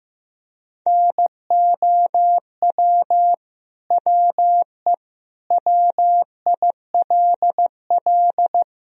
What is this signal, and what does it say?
Morse code, heard as a keyed tone, about 15 wpm, 700 Hz. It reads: NOW WE WILL
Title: Sentences from Top 1000 Words - Review Part 1 - 15wpm